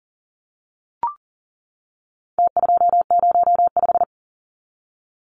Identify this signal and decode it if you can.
Morse code, heard as a keyed tone, about 40 wpm, 700 Hz. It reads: T205
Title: Numbers - Contest Style - 4-Digit 40wpm